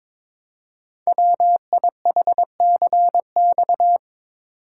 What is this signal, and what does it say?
Morse code, heard as a keyed tone, about 22 wpm, 700 Hz. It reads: WIHCX
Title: Code Groups - 5 Letters - 22wpm